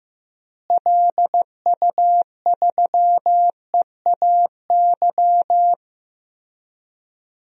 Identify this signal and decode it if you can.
Morse code, heard as a keyed tone, about 15 wpm, 700 Hz. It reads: LU3EAY
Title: DXCC-100 Call Signs - 6 Characters - Repeated 3x - 15wpm